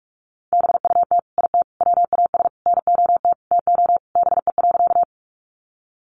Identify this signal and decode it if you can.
Morse code, heard as a keyed tone, about 44 wpm, 700 Hz. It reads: BUT IT WAS NOT TO BE.
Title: Sentences from Top 600 Words 45wpm